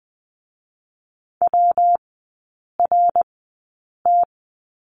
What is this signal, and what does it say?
Morse code, heard as a keyed tone, about 20 wpm, spaced as for 8 wpm, 700 Hz. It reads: WRT
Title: 127 Lesson 21e - Number 6 - Three Characters 20wpm